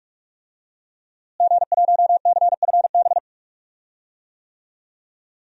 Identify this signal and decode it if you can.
Morse code, heard as a keyed tone, about 46 wpm, 700 Hz. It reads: G1CFB
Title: DXCC-100 Call Signs - 5 Characters - Repeated 2x - 45wpm